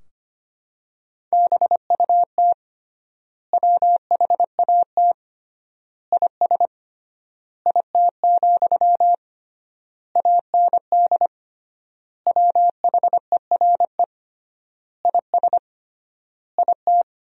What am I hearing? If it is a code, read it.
Morse code, heard as a keyed tone, about 25 wpm, 700 Hz. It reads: BUT WHAT IS IT, AND WHERE IS IT
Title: 3x Word-Spacing - Sentences from Top 300 Words - 25wpm